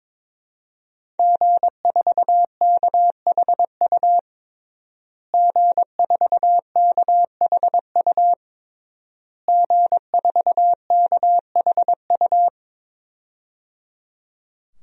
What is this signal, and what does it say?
Morse code, heard as a keyed tone, about 22 wpm, 700 Hz. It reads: G4KHU G4KHU G4KHU
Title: DXCC-100 Call Signs - 5 Characters - Repeated 3x - 22wpm